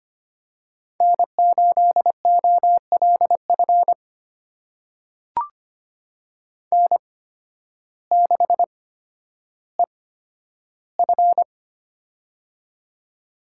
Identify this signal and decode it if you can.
Morse code, heard as a keyed tone, about 25 wpm, spaced as for 12 wpm, 700 Hz. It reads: N8OLF N6EF
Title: USA Call Signs - 8x Character Spacing - 25wpm